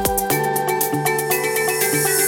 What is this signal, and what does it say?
There is no code.